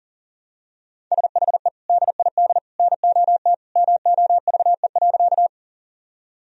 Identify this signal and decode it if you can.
Morse code, heard as a keyed tone, about 40 wpm, 700 Hz. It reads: SHE DID NOT MOVE.
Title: Sentences from Top 1000 Words - Review Part 1 - 40wpm